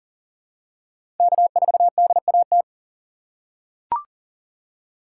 Morse code, heard as a keyed tone, about 40 wpm, 700 Hz. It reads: K4DAT
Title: USA Call Signs - 5 Characters - 3x Character Spacing - 40wpm